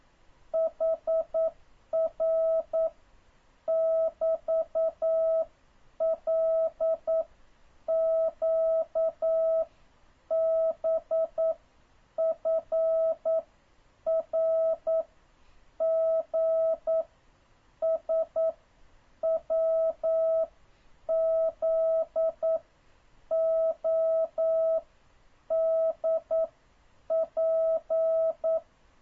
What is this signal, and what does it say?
Morse code, heard as a keyed tone, about 9 wpm, spaced as for 7 wpm, 650 Hz. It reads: HR <BT> LQBFRGSWZODP
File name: jh8cbh / 2026cw